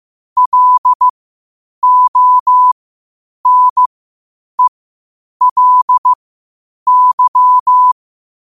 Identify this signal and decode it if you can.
Morse code, heard as a keyed tone, about 15 wpm, spaced as for 8 wpm, 1000 Hz. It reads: LONELY
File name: qotdfiles / 8wpm